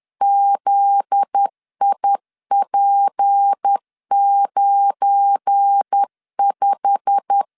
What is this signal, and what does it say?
Morse code, heard as a keyed tone, about 10 wpm, 800 Hz. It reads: ZIP95
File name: LEZIONI / 10 SIGNS / 10B INT 10wpm 2844